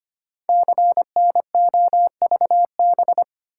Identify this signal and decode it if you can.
Morse code, heard as a keyed tone, about 25 wpm, 700 Hz. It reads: CNOVB
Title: Code Groups - 5 Letters - 25wpm